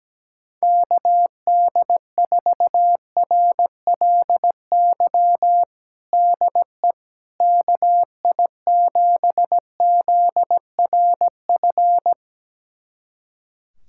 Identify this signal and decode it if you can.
Morse code, heard as a keyed tone, about 17 wpm, 700 Hz. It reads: KD4RLY DE KI7ZRF
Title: USA Callsign DE Callsign 17wpm